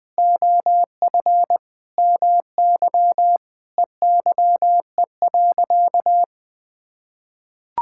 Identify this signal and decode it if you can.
Morse code, heard as a keyed tone, about 20 wpm, 700 Hz. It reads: OF MY EYE.
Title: Sentences with 7-Word Phrases - 20wpm